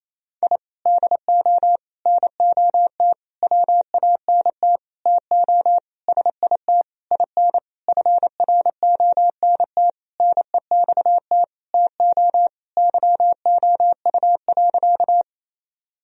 Morse code, heard as a keyed tone, about 28 wpm, 700 Hz. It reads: I DO NOT WANT TO SIT IN FRONT NEXT TO YOU.
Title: Sentences from Top 1000 Words - Review Part 2 - 28wpm